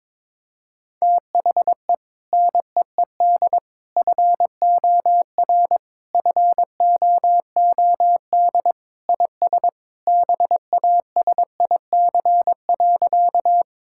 Morse code, heard as a keyed tone, about 22 wpm, 700 Hz. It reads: THE NEED FOR FOOD IS BASIC.
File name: practice / 22wpm / Sentences with 3-Word Phrases - 22wpm